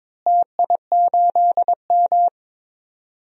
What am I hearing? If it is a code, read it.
Morse code, heard as a keyed tone, about 22 wpm, 700 Hz. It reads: TI8M
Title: DXCC-100 Call Signs - 4 Characters - Repeated 2x - 22wpm